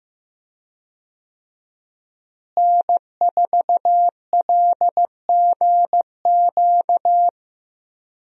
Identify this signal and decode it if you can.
Morse code, heard as a keyed tone, about 15 wpm, 700 Hz. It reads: N4LGQ
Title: USA Call Signs - 5 Characters - Repeated 2x - 15wpm